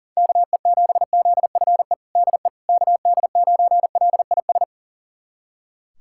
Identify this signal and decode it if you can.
Morse code, heard as a keyed tone, about 40 wpm, 700 Hz. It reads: KE7ZFE DE KD9LIS
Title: USA Callsign DE Callsign 40wpm